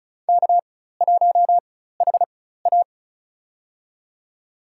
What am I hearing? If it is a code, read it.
Morse code, heard as a keyed tone, about 34 wpm, spaced as for 16 wpm, 700 Hz. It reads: K1HA
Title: USA Call Signs - 4 Characters - 4x Character Spacing - 35wpm